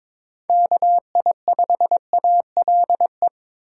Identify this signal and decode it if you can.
Morse code, heard as a keyed tone, about 26 wpm, 700 Hz. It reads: KI5ALE
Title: Speed Racing - USA 2x3 Call Signs 22wpm